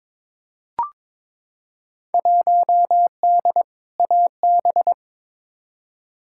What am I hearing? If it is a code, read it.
Morse code, heard as a keyed tone, about 23 wpm, 700 Hz. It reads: 1D AB
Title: Field Day Exchanges Encore - 22wpm